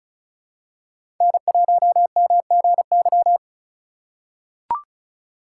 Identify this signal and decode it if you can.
Morse code, heard as a keyed tone, about 34 wpm, 700 Hz. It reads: N1MGY
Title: USA Call Signs - 5 Characters - 2x Character Spacing - 35wpm